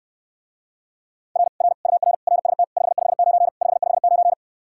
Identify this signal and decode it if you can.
Morse code, heard as a keyed tone, about 69 wpm, 700 Hz. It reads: R R UR RST 559 559